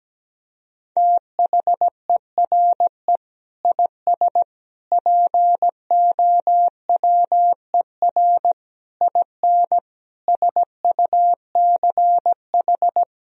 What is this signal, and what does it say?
Morse code, heard as a keyed tone, about 17 wpm, 700 Hz. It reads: THERE IS POWER IN SUCH
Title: Sentences from Top 1000 Words - Review Part 4 - 17wpm